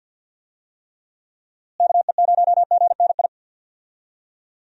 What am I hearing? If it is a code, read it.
Morse code, heard as a keyed tone, about 52 wpm, 700 Hz. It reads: KE0GNI
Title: USA Call Signs - 6 Characters - Repeated 2x - 50wpm